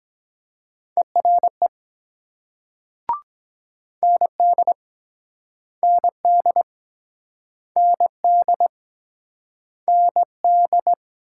Morse code, heard as a keyed tone, about 23 wpm, 700 Hz. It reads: ERE ND ND ND ND